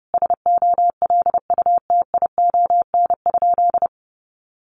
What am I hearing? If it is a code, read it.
Morse code, heard as a keyed tone, about 28 wpm, 700 Hz. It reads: SOLUTION?
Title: Sentences from Top 1300 Words 30wpm